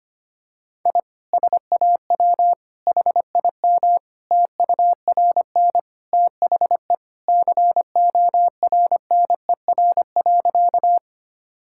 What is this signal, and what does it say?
Morse code, heard as a keyed tone, about 25 wpm, 700 Hz. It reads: I SAW HIM TURN THE CORNER.
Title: Sentences from Top 1600 Words 25wpm